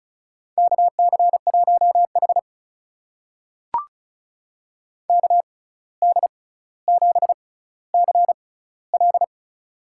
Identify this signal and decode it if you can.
Morse code, heard as a keyed tone, about 36 wpm, spaced as for 17 wpm, 700 Hz. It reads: KC1H KD7CL
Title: USA Call Signs - 6x Character Spacing - 35wpm